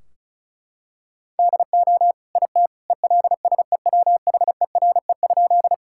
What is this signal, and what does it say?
Morse code, heard as a keyed tone, about 35 wpm, 700 Hz. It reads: DO IT ELSEWHERE?
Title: Sentences from Top 1600 Words 35wpm